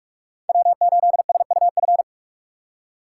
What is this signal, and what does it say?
Morse code, heard as a keyed tone, about 46 wpm, 700 Hz. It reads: W8SUF
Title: USA Call Signs - 3x Character Spacing - 45wpm